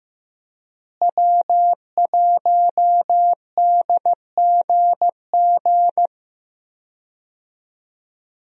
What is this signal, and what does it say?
Morse code, heard as a keyed tone, about 15 wpm, 700 Hz. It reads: W1DGG
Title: USA Call Signs - 5 Characters - Repeated 3x - 15wpm